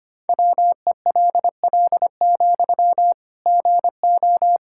Morse code, heard as a keyed tone, about 25 wpm, 700 Hz. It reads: WELL, GO
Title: Sentences from Top 1600 Words 25wpm